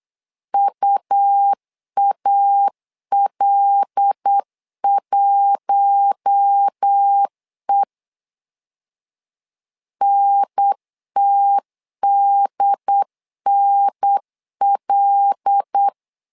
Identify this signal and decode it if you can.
Morse code, heard as a keyed tone, about 8 wpm, 800 Hz. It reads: UAL1E NTDNL
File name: LEZIONI / 5 LBVC / 5 L 8wpm 2831